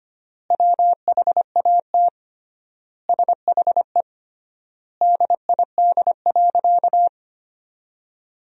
Morse code, heard as a keyed tone, about 25 wpm, 700 Hz. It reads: WHAT SHE DID.
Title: 3x Word-Spacing - Sentences from Top 100 Words - 25wpm